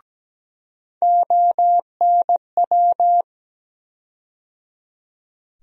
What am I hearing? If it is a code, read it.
Morse code, heard as a keyed tone, about 17 wpm, 700 Hz. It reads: ONW
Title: USA Call Sign Postfixes - 3 Letters 17wpm